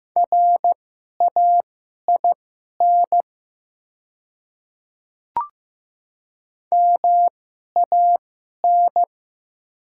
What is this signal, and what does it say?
Morse code, heard as a keyed tone, about 15 wpm, spaced as for 11 wpm, 700 Hz. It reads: RAIN MAN
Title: 2x Character Spacing - Top 300 Words - 15wpm